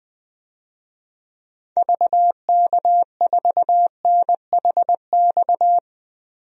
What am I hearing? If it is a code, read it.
Morse code, heard as a keyed tone, about 20 wpm, 700 Hz. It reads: VK4NHX
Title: DXCC-100 Call Signs - 6 Characters - 20wpm